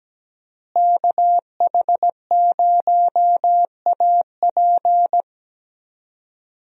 Morse code, heard as a keyed tone, about 17 wpm, 700 Hz. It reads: KH0AP
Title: USA Call Signs - 5 Characters - Repeated 3x - 17wpm